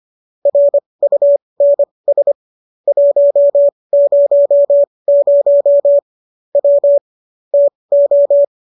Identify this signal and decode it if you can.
Morse code, heard as a keyed tone, about 25 wpm, spaced as for 20 wpm, 570 Hz. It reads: RUNS 100 W TO